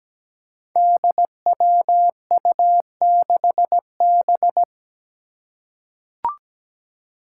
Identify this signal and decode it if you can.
Morse code, heard as a keyed tone, about 17 wpm, 700 Hz. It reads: DWU6B
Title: Five Challenging Letter-Number Combinations 17wpm